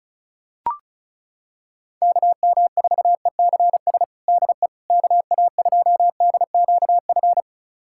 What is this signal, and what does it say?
Morse code, heard as a keyed tone, about 34 wpm, 700 Hz. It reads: KM4ECS DE KA2DQF
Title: USA Callsign DE Callsign - 6 Characters - 35wpm